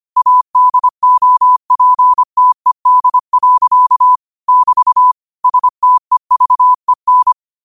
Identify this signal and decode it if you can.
Morse code, heard as a keyed tone, about 25 wpm, 1000 Hz. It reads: ADOPTED. <BT> STEVEN